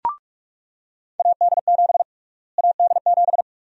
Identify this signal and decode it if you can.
Morse code, heard as a keyed tone, about 46 wpm, 700 Hz. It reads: AD7 AD7
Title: USA Call Sign Prefixes - Repeated 2x - 45wpm